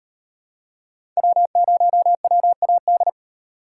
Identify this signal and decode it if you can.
Morse code, heard as a keyed tone, about 35 wpm, 700 Hz. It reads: W0WAD